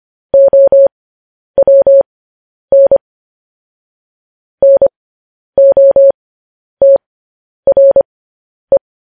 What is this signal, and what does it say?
Morse code, heard as a keyed tone, about 25 wpm, spaced as for 10 wpm, 550 Hz. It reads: OWN NOTRE